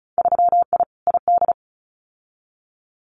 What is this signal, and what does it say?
Morse code, heard as a keyed tone, about 34 wpm, 700 Hz. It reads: 3I ID
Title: Winter Field Day Exchanges 35wpm